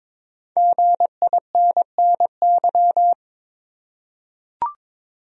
Morse code, heard as a keyed tone, about 22 wpm, 700 Hz. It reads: GINNY
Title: USA Common Names - Max 5 Letters - 22wpm